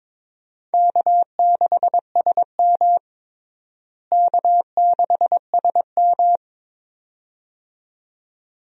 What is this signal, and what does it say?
Morse code, heard as a keyed tone, about 22 wpm, 700 Hz. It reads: K6SM K6SM
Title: USA Call Signs - 4 Characters - Repeated 2x - 22wpm